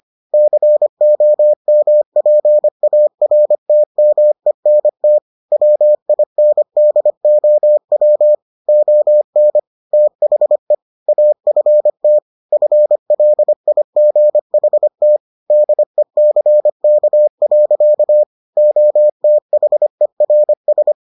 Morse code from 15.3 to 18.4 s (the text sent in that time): DECK.